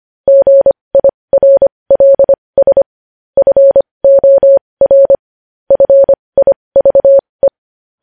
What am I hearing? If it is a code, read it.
Morse code, heard as a keyed tone, about 25 wpm, spaced as for 20 wpm, 550 Hz. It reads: GIRLS FOR FIVE
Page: News Headlines in Morse Code at 20 WPM